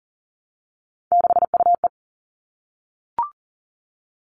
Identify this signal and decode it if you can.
Morse code, heard as a keyed tone, about 40 wpm, 700 Hz. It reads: 6UE
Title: All Cut Numbers - 3-Digit 40wpm